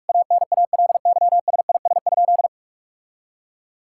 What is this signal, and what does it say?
Morse code, heard as a keyed tone, about 46 wpm, 700 Hz. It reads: ANALYSIS?
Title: Sentences from Top 1000 Words - Review Part 5 - 45wpm